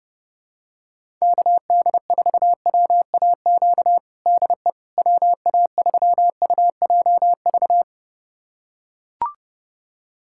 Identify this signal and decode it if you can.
Morse code, heard as a keyed tone, about 30 wpm, 700 Hz. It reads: KD4WAQ DE WA3UJV
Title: USA Callsign DE Callsign 30wpm